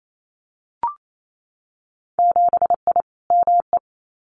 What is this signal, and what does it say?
Morse code, heard as a keyed tone, about 28 wpm, 700 Hz. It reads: 7I ME
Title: Winter Field Day Exchanges Encore 28wpm